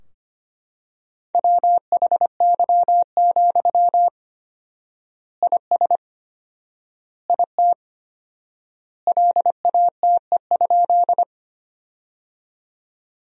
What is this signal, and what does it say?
Morse code, heard as a keyed tone, about 25 wpm, 700 Hz. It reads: WHY, IS IT LATE?
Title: 4x Word-Spacing - Sentences from Top 300 Words - 25wpm